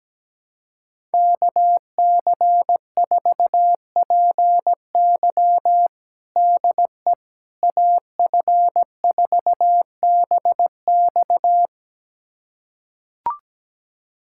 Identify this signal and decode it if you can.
Morse code, heard as a keyed tone, about 17 wpm, 700 Hz. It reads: KC4PY DE AF4BX
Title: USA Callsign DE Callsign - 5 Characters - 17wpm